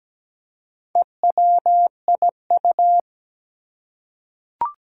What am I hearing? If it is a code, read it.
Morse code, heard as a keyed tone, about 17 wpm, 700 Hz. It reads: EWIU
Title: Four Challenging Letter-Number Combinations 17wpm